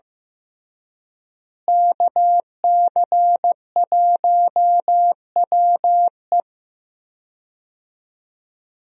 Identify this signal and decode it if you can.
Morse code, heard as a keyed tone, about 16 wpm, 700 Hz. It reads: KC1WE
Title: USA Call Signs - 5 Characters - Repeated 2x - 15wpm